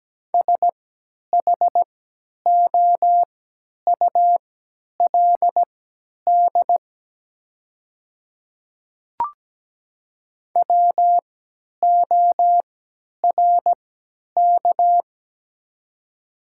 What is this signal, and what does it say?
Morse code, heard as a keyed tone, about 17 wpm, spaced as for 10 wpm, 700 Hz. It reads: SHOULD WORK
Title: 3x Character Spacing - Top 300 Words - 17wpm